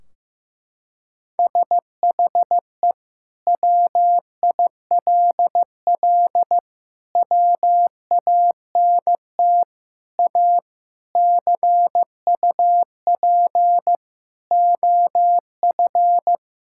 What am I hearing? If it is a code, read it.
Morse code, heard as a keyed tone, about 16 wpm, 700 Hz. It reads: SHE WILL WANT A CUP OF
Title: Sentences from Top 1600 Words 15wpm